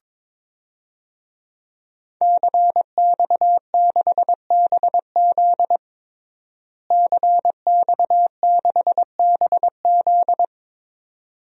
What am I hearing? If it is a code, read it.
Morse code, heard as a keyed tone, about 22 wpm, 700 Hz. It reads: CX6BZ CX6BZ
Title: DXCC-100 Call Signs - 5 Characters - Repeated 2x - 22wpm